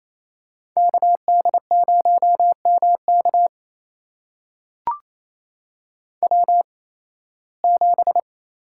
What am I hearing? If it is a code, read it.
Morse code, heard as a keyed tone, about 28 wpm, spaced as for 18 wpm, 700 Hz. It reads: KD0MK W7
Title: USA Call Signs - 5 Characters - 8x Character Spacing - 28wpm